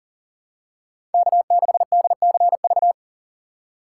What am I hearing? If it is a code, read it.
Morse code, heard as a keyed tone, about 40 wpm, 700 Hz. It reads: K6DCV
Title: USA Call Signs - 6x Character Spacing - 40wpm